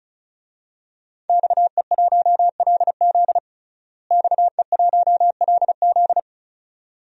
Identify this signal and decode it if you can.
Morse code, heard as a keyed tone, about 34 wpm, 700 Hz. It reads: XE1LZ XE1LZ
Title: DXCC-100 Call Signs - 5 Characters - Repeated 2x - 35wpm